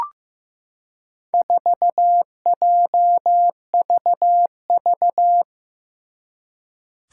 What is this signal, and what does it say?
Morse code, heard as a keyed tone, about 15 wpm, 700 Hz. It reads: 4JVV
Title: Four Challenging Letter-Number Combinations 15wpm